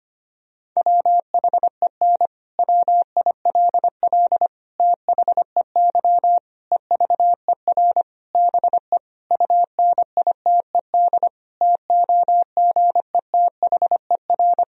Morse code, heard as a keyed tone, about 25 wpm, 700 Hz. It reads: WHEN WILL THEY EVER BE UNITED TOGETHER
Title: Sentences from Top 1000 Words - Review Part 4 - 25wpm